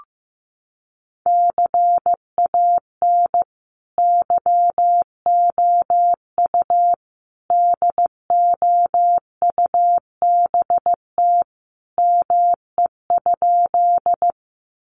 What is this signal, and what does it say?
Morse code, heard as a keyed tone, about 15 wpm, 700 Hz. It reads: CAN YOU DOUBT ME?
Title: Sentences from Top 1500 Words 15wpm